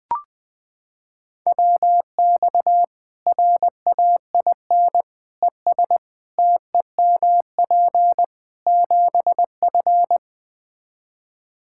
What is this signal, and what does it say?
Morse code, heard as a keyed tone, about 20 wpm, 700 Hz. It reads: WX RAIN ES TEMP 7F
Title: Practice QSO - Exchange 2 - Info - 20wpm